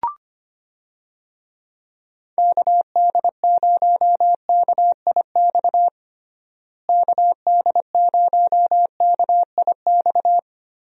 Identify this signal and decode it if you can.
Morse code, heard as a keyed tone, about 25 wpm, 700 Hz. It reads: KD0KIX KD0KIX
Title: USA Call Signs - 6 Characters - Repeated 3x - 25wpm